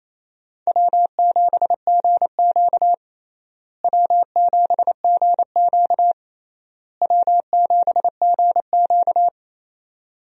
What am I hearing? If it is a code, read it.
Morse code, heard as a keyed tone, about 28 wpm, 700 Hz. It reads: W7GQ W7GQ W7GQ
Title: USA Call Signs - 4 Characters - Repeated 3x - 28wpm